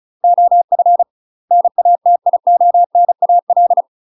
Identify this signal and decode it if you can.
Morse code, heard as a keyed tone, about 34 wpm, 700 Hz. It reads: OF NATIONAL
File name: UK MASTER QSOs TRAINING/35WPM